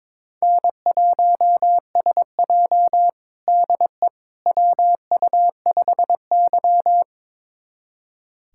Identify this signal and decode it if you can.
Morse code, heard as a keyed tone, about 22 wpm, 700 Hz. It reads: N1SJ DE WU5Y
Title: USA Callsign DE Callsign - 4 Characters - 22wpm